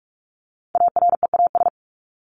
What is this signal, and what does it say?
Morse code, heard as a keyed tone, about 44 wpm, 700 Hz. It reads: AREAS
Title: ES:S Suffix - Top 100 Words with Base Words 45wpm